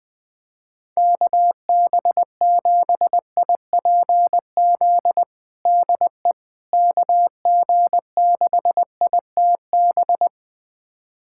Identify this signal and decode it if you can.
Morse code, heard as a keyed tone, about 20 wpm, 700 Hz. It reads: KB7IPZ DE KG6ITB
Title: USA Callsign DE Callsign 20wpm